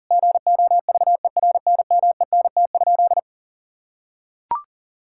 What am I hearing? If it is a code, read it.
Morse code, heard as a keyed tone, about 40 wpm, 700 Hz. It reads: GOVERNMENT?
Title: Sentences from Top 1000 Words - Review Part 6 - 40wpm